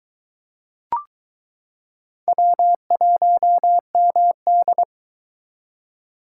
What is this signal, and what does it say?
Morse code, heard as a keyed tone, about 22 wpm, 700 Hz. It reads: W1MD